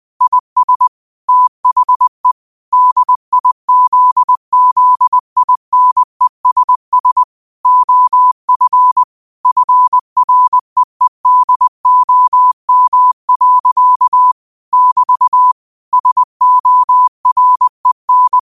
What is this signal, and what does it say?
Morse code, heard as a keyed tone, about 20 wpm, 1000 Hz. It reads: IS THE DIZZINESS OF FREEDOM. <BT> SOREN
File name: qotdfiles / 20wpm